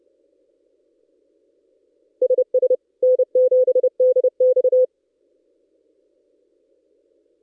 Morse code, heard as a keyed tone, about 29 wpm, 510 Hz. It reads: SS N7DX
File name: contests / ss-cw-k5tr-op-n5rz -1700